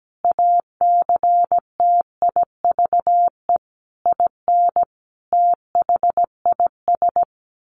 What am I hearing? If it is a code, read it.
Morse code, heard as a keyed tone, about 17 wpm, 700 Hz. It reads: ACTIVE IN THIS
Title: Sentences from Top 1500 Words 17wpm